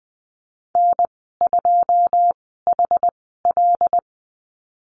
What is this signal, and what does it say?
Morse code, heard as a keyed tone, about 20 wpm, spaced as for 15 wpm, 700 Hz. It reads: N2HL
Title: 088 Lesson 15g - Number 5 - Call Signs 20wpm